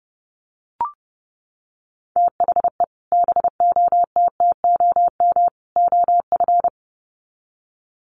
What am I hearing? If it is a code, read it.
Morse code, heard as a keyed tone, about 30 wpm, 700 Hz. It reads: THE BOTTOM OF
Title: Sets of 3 Words 30wpm